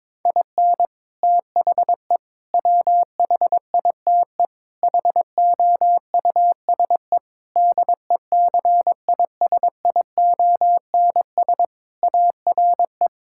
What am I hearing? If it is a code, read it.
Morse code, heard as a keyed tone, about 22 wpm, 700 Hz. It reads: IN THE WHITE HOUSE DECISIONS ARE
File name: practice / 22wpm / Sentences with 4-Word Phrases - 22wpm